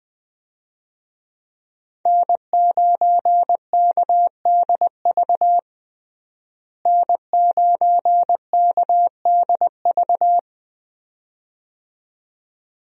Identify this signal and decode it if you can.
Morse code, heard as a keyed tone, about 20 wpm, 700 Hz. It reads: N9KDV N9KDV
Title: USA Call Signs - 5 Characters - Repeated 2x - 20wpm